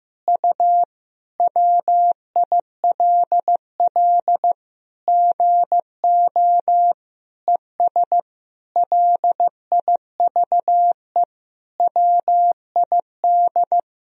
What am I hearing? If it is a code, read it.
Morse code, heard as a keyed tone, about 16 wpm, 700 Hz. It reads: U WILL GO ES LIVE WID
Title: Sentences from Top 500 Words and Abbreviations 15wpm